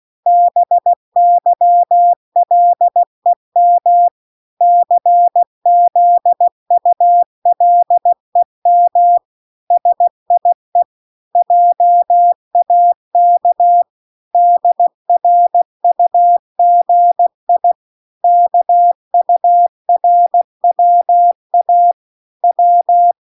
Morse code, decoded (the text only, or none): BYLEM CZULEM SIE JAK DRUGI KURWA W